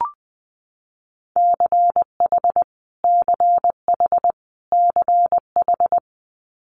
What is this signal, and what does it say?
Morse code, heard as a keyed tone, about 20 wpm, 700 Hz. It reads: CH CH CH
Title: Multi-letter Phonemes Repeated 3x 20wpm